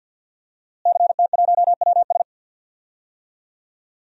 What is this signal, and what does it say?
Morse code, heard as a keyed tone, about 47 wpm, 700 Hz. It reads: CT1WS
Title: DXCC-100 Call Signs - 5 Characters - Repeated 2x - 50wpm